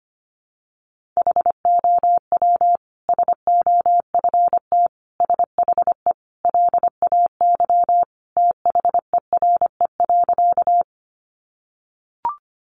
Morse code, heard as a keyed tone, about 25 wpm, 700 Hz. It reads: HOW SOFT SHE LAY THERE.
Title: Sentences from Top 1500 Words 25wpm